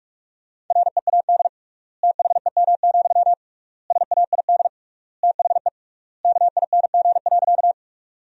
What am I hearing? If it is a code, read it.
Morse code, heard as a keyed tone, about 46 wpm, 700 Hz. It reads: READ THEM, SAID THE KING.
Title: 3x Word-Spacing - Sentences from Top 400 Words - 45wpm